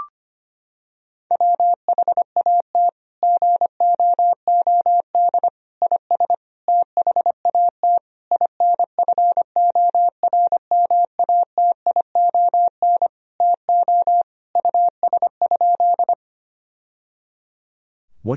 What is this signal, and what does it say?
Morse code, heard as a keyed tone, about 25 wpm, 700 Hz. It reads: WHAT GOOD IS THAT INFORMATION TO US?
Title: Sentences from Top 1000 Words - Review Part 3 - 25wpm